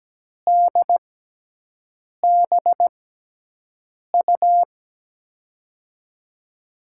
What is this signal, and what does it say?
Morse code, heard as a keyed tone, about 17 wpm, spaced as for 6 wpm, 700 Hz. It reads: DBU
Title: USA Call Signs - 6 Characters - 6x Character Spacing - 17wpm